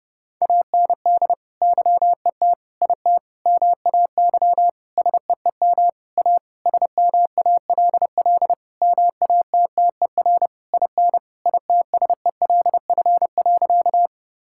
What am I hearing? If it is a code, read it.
Morse code, heard as a keyed tone, about 30 wpm, 700 Hz. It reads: AND YET IT MAY SEEM A SMALL MATTER IN ITSELF.
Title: Sentences from Top 1000 Words - Review Part 5 - 30wpm